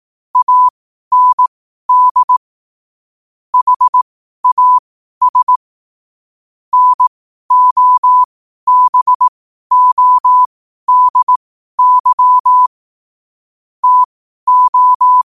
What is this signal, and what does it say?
Morse code, heard as a keyed tone, about 18 wpm, spaced as for 12 wpm, 1000 Hz. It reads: AND HAS NOBODY TO
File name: qotdfiles / 12wpm